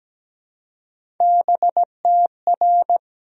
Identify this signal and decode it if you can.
Morse code, heard as a keyed tone, about 17 wpm, 700 Hz. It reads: BTR
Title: Sentences from Top 400 Words and Abbreviations 17wpm